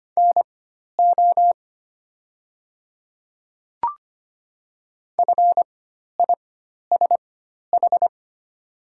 Morse code, heard as a keyed tone, about 25 wpm, spaced as for 12 wpm, 700 Hz. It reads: NO FISH
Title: 4x Character Spacing - Top 300 Words - 25wpm